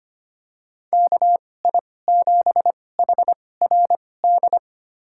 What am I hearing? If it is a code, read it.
Morse code, heard as a keyed tone, about 25 wpm, spaced as for 18 wpm, 700 Hz. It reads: KI7HRD
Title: USA Call Signs - 2x Character Spacing - 25wpm